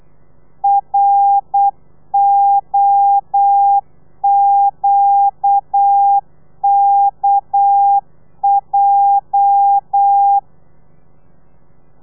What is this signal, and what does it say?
Morse code, heard as a keyed tone, about 8 wpm, 800 Hz. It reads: ROQKJ
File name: telegrafia / i2tfj / Lez9 35